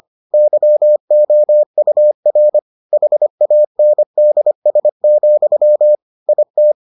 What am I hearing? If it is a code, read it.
Morse code, heard as a keyed tone, about 25 wpm, 600 Hz. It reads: YOUR HANDS, IT